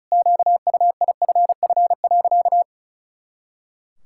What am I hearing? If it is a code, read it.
Morse code, heard as a keyed tone, about 34 wpm, 700 Hz. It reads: QUIFF.
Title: Pangram Challenge v2 - 35wpm